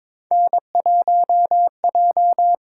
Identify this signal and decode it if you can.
Morse code, heard as a keyed tone, about 22 wpm, 700 Hz. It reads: N1J
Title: USA Call Signs - 8x Character Spacing - 22wpm